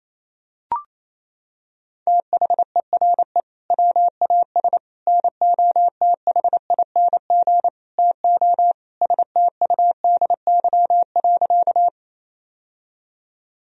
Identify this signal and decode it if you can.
Morse code, heard as a keyed tone, about 28 wpm, 700 Hz. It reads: THERE WAS NOTHING TO STUDY.
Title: Sentences from Top 1000 Words - Review Part 1 - 28wpm